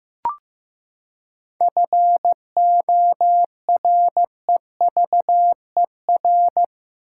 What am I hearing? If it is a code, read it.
Morse code, heard as a keyed tone, about 16 wpm, 700 Hz. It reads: FOREVER
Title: Seven Letter Words - 15wpm